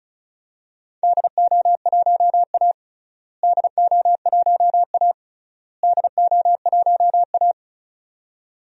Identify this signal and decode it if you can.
Morse code, heard as a keyed tone, about 36 wpm, 700 Hz. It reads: DO1A DO1A DO1A
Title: DXCC-100 Call Signs - 4 Characters - Repeated 3x - 35wpm